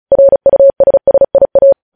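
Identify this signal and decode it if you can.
Morse code, heard as a keyed tone, about 36 wpm, 550 Hz. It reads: RUSSIA
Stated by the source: Morse practice at 35 wpm sending news headlines.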